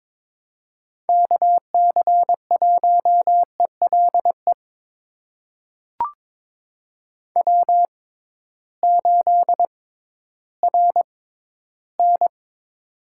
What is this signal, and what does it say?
Morse code, heard as a keyed tone, about 22 wpm, spaced as for 13 wpm, 700 Hz. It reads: KC1ELE W8RN
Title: USA Call Signs - 6x Character Spacing - 22wpm